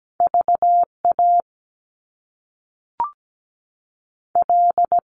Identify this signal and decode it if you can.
Morse code, heard as a keyed tone, about 17 wpm, 700 Hz. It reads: VA L
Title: US States 17wpm